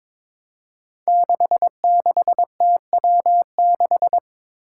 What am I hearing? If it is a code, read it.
Morse code, heard as a keyed tone, about 22 wpm, 700 Hz. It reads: 66TW6
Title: Five Challenging Letter-Number Combinations 22wpm